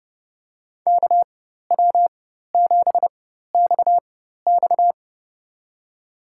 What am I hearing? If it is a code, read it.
Morse code, heard as a keyed tone, about 30 wpm, spaced as for 14 wpm, 700 Hz. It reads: KW7XX
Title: USA Call Signs - 5 Characters - 4x Character Spacing - 30wpm